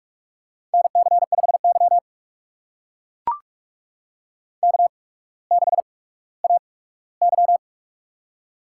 Morse code, heard as a keyed tone, about 44 wpm, spaced as for 19 wpm, 700 Hz. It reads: NC5Y K6AY
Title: USA Call Signs - 4 Characters - 8x Character Spacing - 45wpm